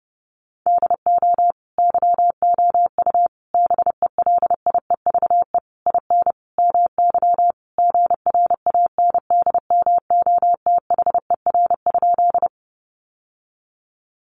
Morse code, heard as a keyed tone, about 30 wpm, 700 Hz. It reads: DO YOU BELIEVE IN MY GRANDMOTHER?
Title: Sentences from Top 1200 Words 30wpm